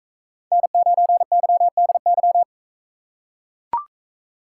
Morse code, heard as a keyed tone, about 39 wpm, 700 Hz. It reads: N9YDY